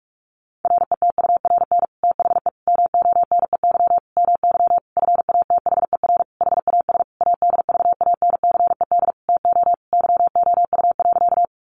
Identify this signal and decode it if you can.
Morse code, heard as a keyed tone, about 44 wpm, 700 Hz. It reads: RETURN THE MONEY MY FATHER HAS ADVANCED TO YOU.
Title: Sentences from Top 1200 Words 45wpm